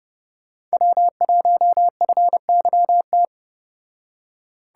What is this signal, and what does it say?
Morse code, heard as a keyed tone, about 34 wpm, 700 Hz. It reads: W1FYT